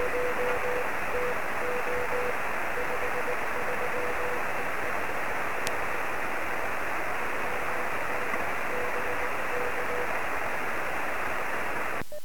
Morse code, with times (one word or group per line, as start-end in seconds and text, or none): none